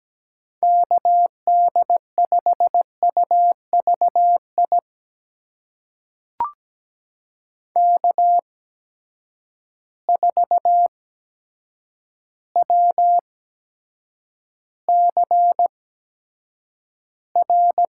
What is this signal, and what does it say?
Morse code, heard as a keyed tone, about 17 wpm, spaced as for 8 wpm, 700 Hz. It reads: KD5UVI K4WCR
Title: USA Call Signs - 8x Character Spacing - 17wpm